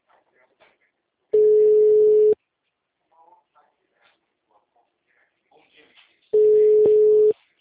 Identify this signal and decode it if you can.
Ringback tone (Europe, 425 Hz), with speech.